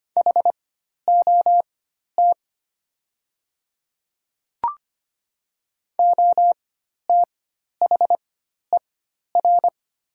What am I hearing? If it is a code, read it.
Morse code, heard as a keyed tone, about 25 wpm, spaced as for 12 wpm, 700 Hz. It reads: HOT OTHER
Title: 4x Character Spacing - Top 100 Words - 25wpm